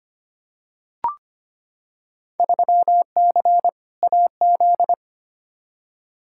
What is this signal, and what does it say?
Morse code, heard as a keyed tone, about 25 wpm, 700 Hz. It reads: 3C AZ
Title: Field Day Exchanges Encore - 25wpm